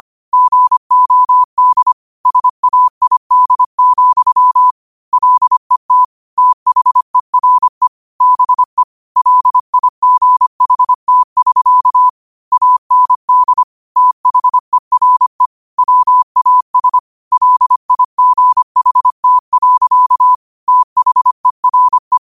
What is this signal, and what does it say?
Morse code, heard as a keyed tone, about 25 wpm, 1000 Hz. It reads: GOD SAID, LET THERE BE LIGHT<SK> AND THERE WAS LIGHT. THERE